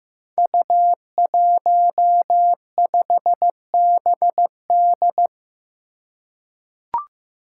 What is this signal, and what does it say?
Morse code, heard as a keyed tone, about 15 wpm, 700 Hz. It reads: U15BD
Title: Five Challenging Letter-Number Combinations 15wpm